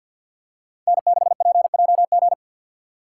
Morse code, heard as a keyed tone, about 51 wpm, 700 Hz. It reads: N6PJG
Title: USA Call Signs - 5 Characters - 2x Character Spacing - 50wpm